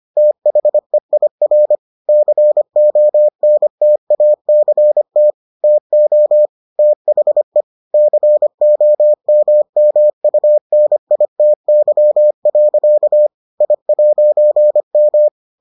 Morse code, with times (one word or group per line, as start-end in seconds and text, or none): 0.2-1.8 THEIR
2.1-5.3 CONTACT
5.6-6.5 TO
6.8-7.6 THE
7.9-13.3 COMMUNITY.
13.6-15.3 I'M